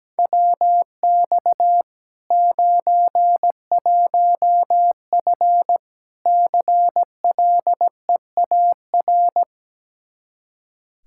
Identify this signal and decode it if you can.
Morse code, heard as a keyed tone, about 17 wpm, 700 Hz. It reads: WX 91F CLEAR
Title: Practice QSO - Exchange 2 - Info - 17wpm